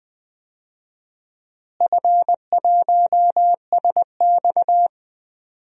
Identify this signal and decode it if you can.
Morse code, heard as a keyed tone, about 21 wpm, 700 Hz. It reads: F1SX
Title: DXCC-100 Call Signs - 4 Characters - Repeated 3x - 20wpm